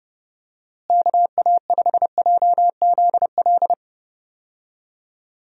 Morse code, heard as a keyed tone, about 32 wpm, 700 Hz. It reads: KA5JZL
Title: USA Call Signs - 6 Characters - Repeated 2x - 30wpm